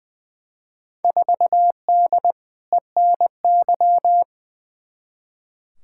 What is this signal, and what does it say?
Morse code, heard as a keyed tone, about 20 wpm, 700 Hz. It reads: 4D ENY
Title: Field Day Exchanges Encore - 20wpm